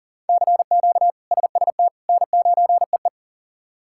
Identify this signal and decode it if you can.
Morse code, heard as a keyed tone, about 37 wpm, 700 Hz. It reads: CQ SST N9EE
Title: K1USN SST QSO Practice - 40wpm